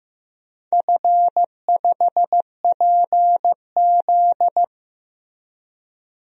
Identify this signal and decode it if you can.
Morse code, heard as a keyed tone, about 15 wpm, 700 Hz. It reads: F5PZ
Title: DXCC-100 Call Signs - 4 Characters - Repeated 3x - 15wpm